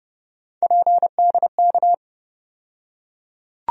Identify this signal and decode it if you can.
Morse code, heard as a keyed tone, about 30 wpm, 700 Hz. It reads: PDK
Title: USA Call Sign Postfixes - 3 Letters 30wpm